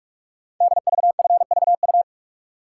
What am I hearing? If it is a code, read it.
Morse code, heard as a keyed tone, about 44 wpm, 700 Hz. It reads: DVFVU
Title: Code Groups - 5 Letters - 45wpm